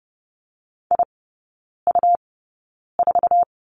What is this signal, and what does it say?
Morse code, heard as a keyed tone, about 30 wpm, spaced as for 9 wpm, 700 Hz. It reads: IU4
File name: Lesson 17e - Letter u - Three Characters 30wpm